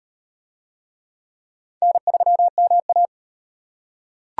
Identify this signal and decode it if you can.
Morse code, heard as a keyed tone, about 38 wpm, 700 Hz. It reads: N3MA